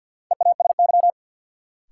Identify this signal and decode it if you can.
Morse code, heard as a keyed tone, about 52 wpm, 700 Hz. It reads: EASY
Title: Sentences from Top 700 Words - New Word Drill - 50wpm